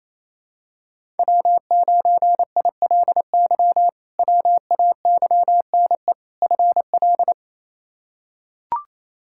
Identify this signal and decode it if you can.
Morse code, heard as a keyed tone, about 28 wpm, 700 Hz. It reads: W9ILY WAYNE FL
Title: K1USN SST QSO Practice - 28wpm